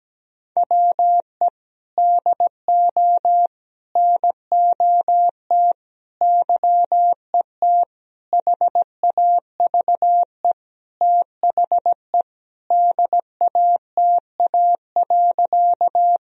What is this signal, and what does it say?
Morse code, heard as a keyed tone, about 17 wpm, 700 Hz. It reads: WE DO NOT YET HAVE THE DATA.
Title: Sentences from Top 1000 Words - Review Part 3 - 17wpm